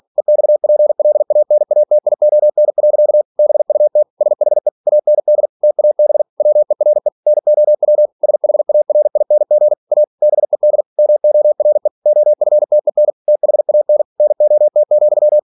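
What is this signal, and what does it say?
Morse code, heard as a keyed tone, about 46 wpm, 600 Hz. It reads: EXPLANATION. BUT SHE AND TAB WERE NOW SHARING A BED MORE OFTEN THAN NOT,
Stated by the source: Practice text from an English novel sent at 47 WPM.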